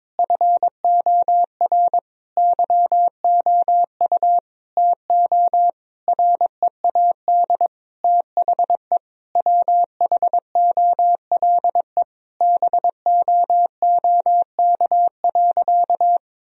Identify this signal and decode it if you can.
Morse code, heard as a keyed tone, about 22 wpm, 700 Hz. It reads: FOR YOU TO READ THE WHOLE BOOK.
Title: Sentences from Top 1000 Words - Review Part 1 - 22wpm